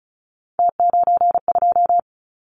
Morse code, heard as a keyed tone, about 36 wpm, 700 Hz. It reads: T92
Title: Callsign Prefixes for Top 100 DXCC - 35wpm